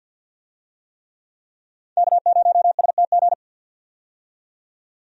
Morse code, heard as a keyed tone, about 48 wpm, 700 Hz. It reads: K0STG
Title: USA Call Signs - 5 Characters - 50wpm